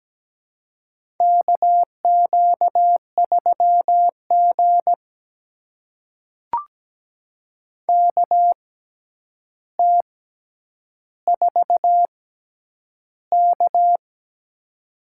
Morse code, heard as a keyed tone, about 17 wpm, spaced as for 9 wpm, 700 Hz. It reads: KQ3G KT4K
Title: USA Call Signs - 4 Characters - 6x Character Spacing - 17wpm